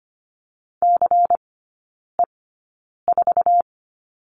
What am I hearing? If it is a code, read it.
Morse code, heard as a keyed tone, about 25 wpm, spaced as for 9 wpm, 700 Hz. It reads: CE4
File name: course / 25:10wpm / 093 Lesson 16e - Letter c - Three Characters 25wpm